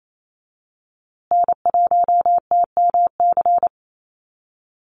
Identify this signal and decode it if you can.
Morse code, heard as a keyed tone, about 28 wpm, 700 Hz. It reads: N1TMC
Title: USA Call Signs Repeated 2x 28wpm